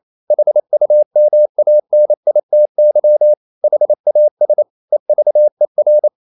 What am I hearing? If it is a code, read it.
Morse code, heard as a keyed tone, about 28 wpm, 600 Hz. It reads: HUMANITY HAS EVER
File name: CW / Practice / EnglishQuotations0128